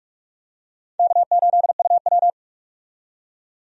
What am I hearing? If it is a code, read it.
Morse code, heard as a keyed tone, about 42 wpm, 700 Hz. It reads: K8UW
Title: USA Call Signs - 4 Characters - 45wpm